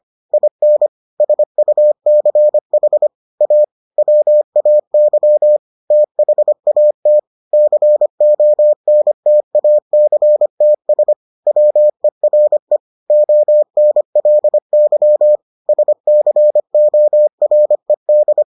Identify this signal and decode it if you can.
Morse code, heard as a keyed tone, about 25 wpm, 600 Hz. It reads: IN SUCH A WAY THAT CONTACTS WERE ONLY SCORED